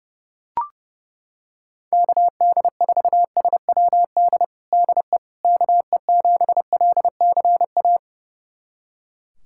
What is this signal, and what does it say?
Morse code, heard as a keyed tone, about 30 wpm, 690 Hz. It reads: KD4SWD DE KE7LCA
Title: USA Callsign DE Callsign - 6 Characters - 30wpm